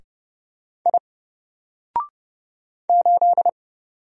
Morse code, heard as a keyed tone, about 30 wpm, 700 Hz. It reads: I 8
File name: Single-Letter-Number-30wpm